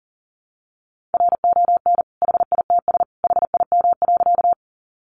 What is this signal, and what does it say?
Morse code, heard as a keyed tone, about 37 wpm, 700 Hz. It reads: RON HITS HIM.
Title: Additional Easy Phrases 40wpm